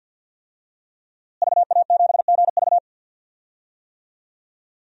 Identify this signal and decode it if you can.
Morse code, heard as a keyed tone, about 50 wpm, 700 Hz. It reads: VA7GV
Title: DXCC-100 Call Signs - 5 Characters - Repeated 2x - 50wpm